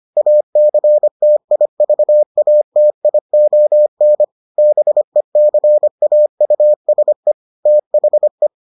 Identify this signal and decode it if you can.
Morse code, heard as a keyed tone, about 25 wpm, 600 Hz. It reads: ACTIVATION BECAUSE THE